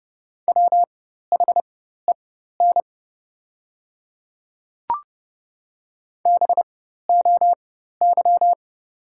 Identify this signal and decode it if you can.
Morse code, heard as a keyed tone, about 30 wpm, spaced as for 14 wpm, 700 Hz. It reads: WHEN BOY
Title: 4x Character Spacing - Top 300 Words - 30wpm